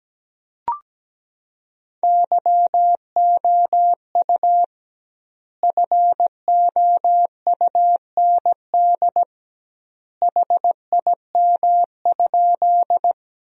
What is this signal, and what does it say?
Morse code, heard as a keyed tone, about 17 wpm, 700 Hz. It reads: YOU FOUND HIM?
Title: 2x Word-Spacing - Sentences from Top 300 Words - 17wpm